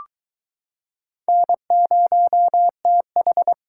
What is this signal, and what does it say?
Morse code, heard as a keyed tone, about 22 wpm, 700 Hz. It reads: N0TH